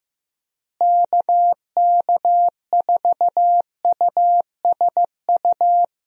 Morse code, heard as a keyed tone, about 15 wpm, 700 Hz. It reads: KK4USU
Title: USA Call Signs - 6 Characters - Repeated 3x - 15wpm